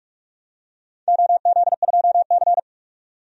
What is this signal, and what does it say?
Morse code, heard as a keyed tone, about 44 wpm, 700 Hz. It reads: OZ2C
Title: DXCC-100 Call Signs - 4 Characters - 45wpm